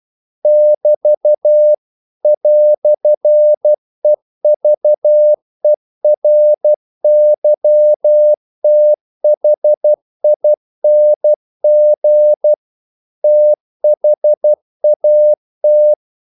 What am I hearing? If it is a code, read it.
Morse code, heard as a keyed tone, about 12 wpm, 600 Hz. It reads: <BT> "EVERYTHING THAT
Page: English quotations in CW (telegraphy - morse)